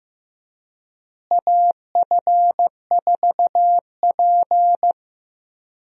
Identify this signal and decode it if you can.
Morse code, heard as a keyed tone, about 15 wpm, 700 Hz. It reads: AF4P
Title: USA Call Signs - 4 Characters - Repeated 3x - 15wpm